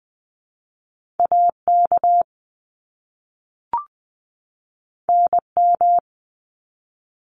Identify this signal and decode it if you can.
Morse code, heard as a keyed tone, about 20 wpm, 700 Hz. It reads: AK NM
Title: ARRL Sections 20wpm